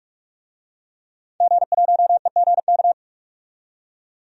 Morse code, heard as a keyed tone, about 44 wpm, 700 Hz. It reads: G1EGK
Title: DXCC-100 Call Signs - 5 Characters - 45wpm